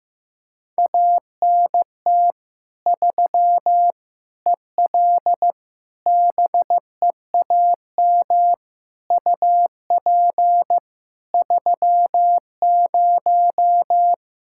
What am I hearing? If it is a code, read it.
Morse code, heard as a keyed tone, about 15 wpm, 700 Hz. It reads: ANT 3 EL BEAM UP 30
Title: Practice QSO - Exchange 2 - Info - 15wpm